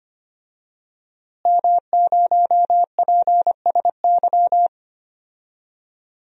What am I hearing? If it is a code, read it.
Morse code, heard as a keyed tone, about 25 wpm, 700 Hz. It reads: M0PSY
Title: DXCC-100 Call Signs - 5 Characters - Repeated 2x - 25wpm